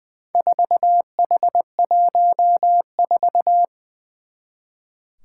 Morse code, heard as a keyed tone, about 20 wpm, 700 Hz. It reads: 4H14
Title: Four Challenging Letter-Number Combinations 20wpm